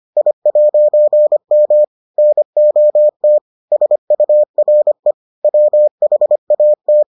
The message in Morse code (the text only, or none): I'M NOT SURE WHAT